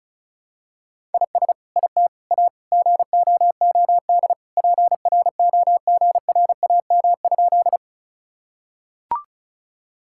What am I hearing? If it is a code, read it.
Morse code, heard as a keyed tone, about 34 wpm, 700 Hz. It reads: IS IT A GOOD PROGRAM?
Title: Sentences from Top 1000 Words - Review Part 5 - 35wpm